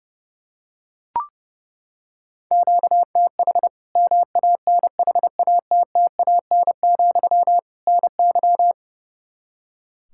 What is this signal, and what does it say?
Morse code, heard as a keyed tone, about 30 wpm, 700 Hz. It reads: QTH MANHATTAN, NY
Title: Practice QSO - Exchange 1 - Beginning - 30wpm